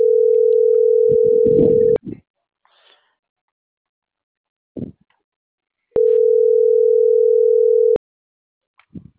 Ringback tone (North America), 440 + 480 Hz.